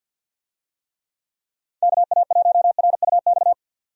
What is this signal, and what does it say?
Morse code, heard as a keyed tone, about 52 wpm, 700 Hz. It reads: KA1RUX
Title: USA Call Signs - 6 Characters - 50wpm